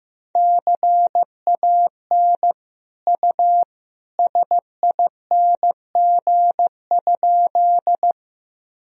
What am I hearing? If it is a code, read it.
Morse code, heard as a keyed tone, about 15 wpm, 700 Hz. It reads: CAN U SING?
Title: Sentences from Top 400 Words and Abbreviations 15wpm